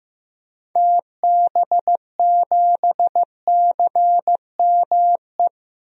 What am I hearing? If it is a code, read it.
Morse code, heard as a keyed tone, about 15 wpm, 700 Hz. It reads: TB7CME
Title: DXCC-100 Call Signs - 6 Characters - 15wpm